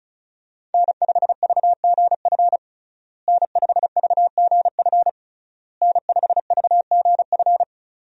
Morse code, heard as a keyed tone, about 36 wpm, 700 Hz. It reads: N5VGF N5VGF N5VGF
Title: USA Call Signs - 5 Characters - Repeated 3x - 35wpm